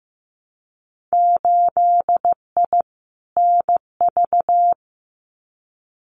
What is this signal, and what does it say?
Morse code, heard as a keyed tone, about 15 wpm, 700 Hz. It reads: 8I NV
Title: Winter Field Day Exchanges Encore 15wpm